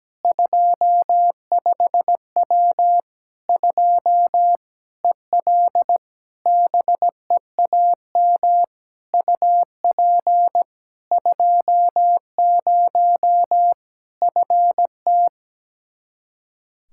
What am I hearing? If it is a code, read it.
Morse code, heard as a keyed tone, about 17 wpm, 700 Hz. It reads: 25W 2 EL BEAM UP 20 FT
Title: Practice QSO - Exchange 2 - Info - 17wpm